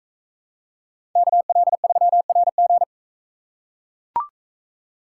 Morse code, heard as a keyed tone, about 41 wpm, 700 Hz. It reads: KL3RG